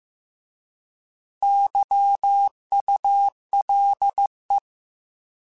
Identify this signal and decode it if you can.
Morse code, heard as a keyed tone, about 15 wpm, 800 Hz. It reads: YULE
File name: Alphabetical / Alphabetical mp3s / Ys @13